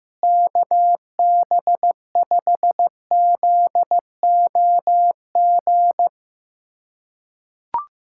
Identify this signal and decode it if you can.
Morse code, heard as a keyed tone, about 15 wpm, 700 Hz. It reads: KB5ZOG
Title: USA Callsign DE Callsign - 6 Characters - 15wpm